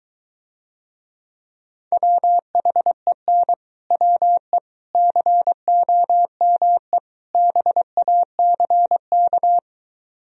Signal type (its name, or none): Morse code